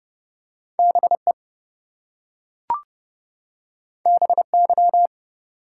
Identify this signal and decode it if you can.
Morse code, heard as a keyed tone, about 30 wpm, 700 Hz. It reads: BE BY
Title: Two Letter Words - 30wpm